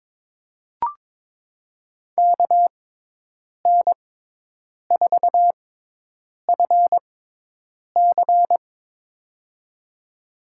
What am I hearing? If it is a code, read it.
Morse code, heard as a keyed tone, about 22 wpm, spaced as for 8 wpm, 700 Hz. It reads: KN4FC
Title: USA Call Signs - 5 Characters - 6x Character Spacing - 22wpm